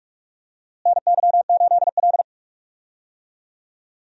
Morse code, heard as a keyed tone, about 46 wpm, 700 Hz. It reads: NY8L
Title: USA Call Signs - 4 Characters - Repeated 2x - 45wpm